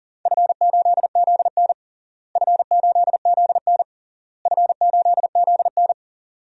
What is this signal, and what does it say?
Morse code, heard as a keyed tone, about 40 wpm, 700 Hz. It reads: F8ZN F8ZN F8ZN
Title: DXCC-100 Call Signs - 4 Characters - Repeated 3x - 40wpm